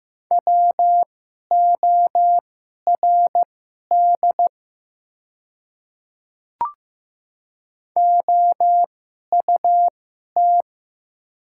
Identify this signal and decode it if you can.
Morse code, heard as a keyed tone, about 15 wpm, spaced as for 11 wpm, 700 Hz. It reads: WORD OUT
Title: 2x Character Spacing - Top 100 Words - 15wpm